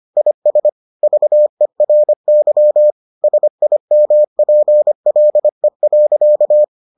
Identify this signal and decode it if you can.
Morse code, heard as a keyed tone, about 25 wpm, 600 Hz. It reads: IS VERY SIMPLE.